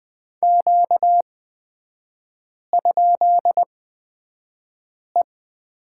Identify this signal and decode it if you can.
Morse code, heard as a keyed tone, about 20 wpm, spaced as for 5 wpm, 700 Hz. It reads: Q?E
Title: 243 Lesson 39e - Letter z - Three Characters 20wpm